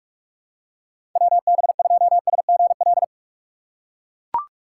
Morse code, heard as a keyed tone, about 42 wpm, 700 Hz. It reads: WB2SGL
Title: USA Call Signs - 6 Characters - 4x Character Spacing - 45wpm